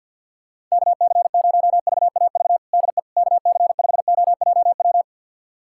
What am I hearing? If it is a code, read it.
Morse code, heard as a keyed tone, about 46 wpm, 700 Hz. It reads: KC0VAV DE KC5OJW
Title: USA Callsign DE Callsign 50wpm